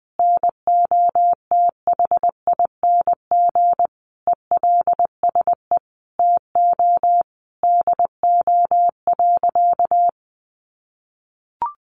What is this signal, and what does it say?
Morse code, heard as a keyed tone, about 20 wpm, 700 Hz. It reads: NOTHING ELSE TO DO.
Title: Sentences from Top 800 Words 20wpm